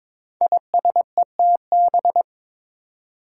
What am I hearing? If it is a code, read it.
Morse code, heard as a keyed tone, about 22 wpm, 700 Hz. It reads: ISETB
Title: Five Challenging Letter-Number Combinations 22wpm